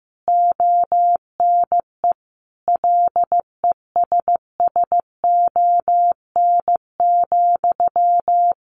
Morse code, heard as a keyed tone, about 15 wpm, 700 Hz. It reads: ONE LESSON,
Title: Sentences from Top 1200 Words 15wpm